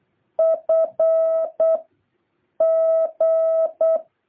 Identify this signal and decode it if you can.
Morse code, heard as a keyed tone, about 8 wpm, 650 Hz. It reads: F G
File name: jh8cbh / 2023cw